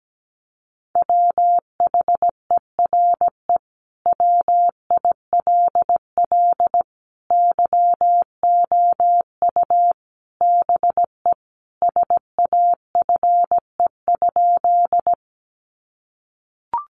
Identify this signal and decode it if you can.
Morse code, heard as a keyed tone, about 17 wpm, 700 Hz. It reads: WHERE WILL YOU BE SAFE?
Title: Sentences from Top 900 Words 17wpm